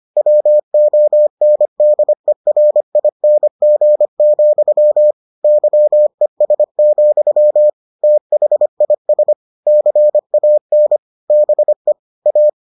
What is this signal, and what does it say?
Morse code, heard as a keyed tone, about 25 wpm, 600 Hz. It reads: WONDERING, YES, THIS CAN BE A